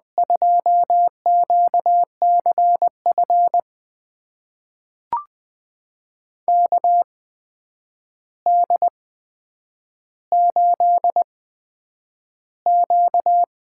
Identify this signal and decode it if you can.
Morse code, heard as a keyed tone, about 20 wpm, spaced as for 9 wpm, 700 Hz. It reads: 2QCF KD8Q
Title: USA Call Signs - 6 Characters - 8x Character Spacing - 20wpm